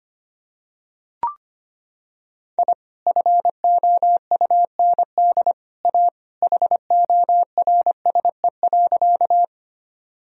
Morse code, heard as a keyed tone, about 25 wpm, 700 Hz. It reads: I FOUND A HORSE.
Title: Sentences from Top 1000 Words - Review Part 1 - 25wpm